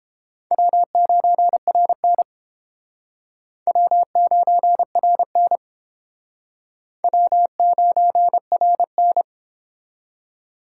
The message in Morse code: W9RN W9RN W9RN

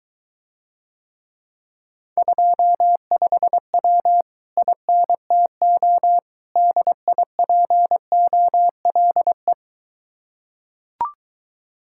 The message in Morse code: 25W INTO DIPOLE

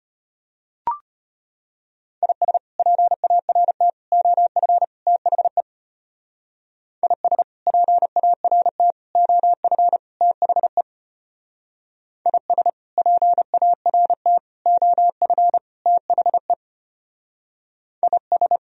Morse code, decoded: IS PART OF THE IS PART OF THE IS PART OF THE IS

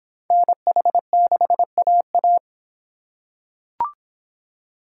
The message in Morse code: NH6AA

